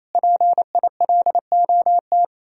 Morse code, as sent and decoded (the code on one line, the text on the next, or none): .--. .. .-.. --- -
PILOT